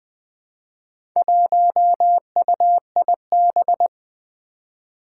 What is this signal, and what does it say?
Morse code, heard as a keyed tone, about 21 wpm, 700 Hz. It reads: 1UIB
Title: Four Challenging Letter-Number Combinations 20wpm